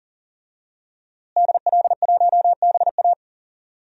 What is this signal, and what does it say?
Morse code, heard as a keyed tone, about 37 wpm, 700 Hz. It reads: DL1BA
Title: DXCC-100 Call Signs - 5 Characters - Repeated 2x - 40wpm